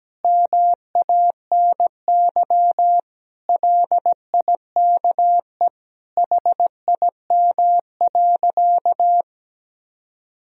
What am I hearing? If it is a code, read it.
Morse code, heard as a keyed tone, about 17 wpm, 700 Hz. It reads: MANY LIKE HIM.
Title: Sentences from Top 1000 Words - Review Part 1 - 17wpm